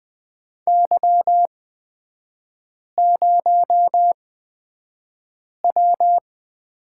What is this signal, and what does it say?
Morse code, heard as a keyed tone, about 20 wpm, spaced as for 5 wpm, 700 Hz. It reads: Y0W
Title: 255 Lesson 41e - Period - Three Characters 20wpm